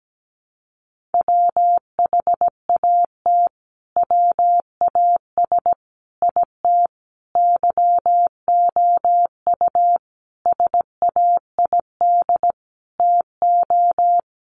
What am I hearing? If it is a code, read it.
Morse code, heard as a keyed tone, about 17 wpm, 700 Hz. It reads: WHAT WAS IT YOU SAID TO
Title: Sentences from Top 200 Words 17wpm